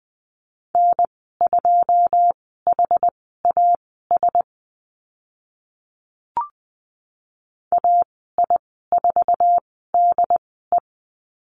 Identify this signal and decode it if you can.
Morse code, heard as a keyed tone, about 20 wpm, spaced as for 15 wpm, 700 Hz. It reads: N2HAS AI4DE
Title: 082 Lesson 14g - Number 2 - Call Signs 20wpm